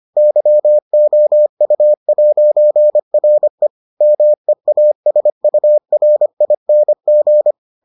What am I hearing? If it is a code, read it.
Morse code, heard as a keyed tone, about 25 wpm, 600 Hz. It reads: YOU'RE MEASURING